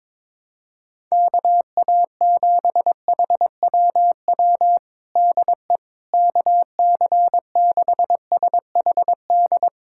Morse code, heard as a keyed tone, about 21 wpm, 700 Hz. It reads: KA7HWW DE KC6SHD
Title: USA Callsign DE Callsign 22wpm